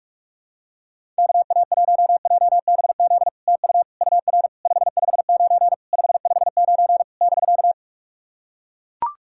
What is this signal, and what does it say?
Morse code, heard as a keyed tone, about 44 wpm, 700 Hz. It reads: KA1JBZ TU UR 559 559 <BK>
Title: SOTA QSO Practice 45wpm